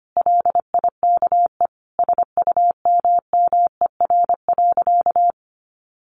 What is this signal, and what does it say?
Morse code, heard as a keyed tone, about 25 wpm, 700 Hz. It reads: LIKE SUMMER.
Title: Sentences from Top 1100 Words 25wpm